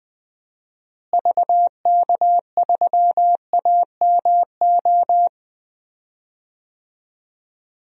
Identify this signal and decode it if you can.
Morse code, heard as a keyed tone, about 20 wpm, 700 Hz. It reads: VK3AMO
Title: DXCC-100 Call Signs - 6 Characters - Repeated 3x - 20wpm